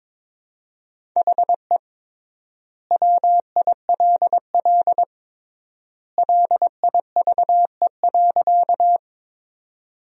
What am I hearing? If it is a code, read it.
Morse code, heard as a keyed tone, about 22 wpm, 700 Hz. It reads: HE WILL LIVE.
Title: 3x Word-Spacing - Sentences from Top 200 Words - 22wpm